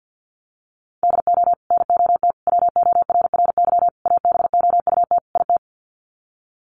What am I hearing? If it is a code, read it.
Morse code, heard as a keyed tone, about 50 wpm, 700 Hz. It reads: DO NOT WORRY ABOUT IT
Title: Sets of 5 Words 50wpm